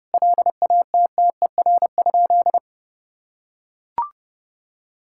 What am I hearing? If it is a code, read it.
Morse code, heard as a keyed tone, about 28 wpm, 700 Hz. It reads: LATTER?
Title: Sentences from Top 1600 Words 30wpm